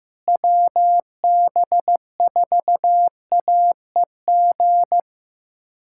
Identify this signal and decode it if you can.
Morse code, heard as a keyed tone, about 15 wpm, 700 Hz. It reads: WB4AEG
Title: K1USN SST QSO Practice - 15wpm